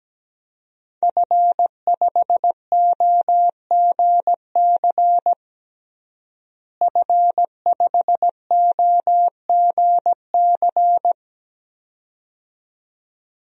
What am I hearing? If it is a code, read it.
Morse code, heard as a keyed tone, about 17 wpm, 700 Hz. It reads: F5OGC F5OGC
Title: DXCC-100 Call Signs - 5 Characters - Repeated 2x - 17wpm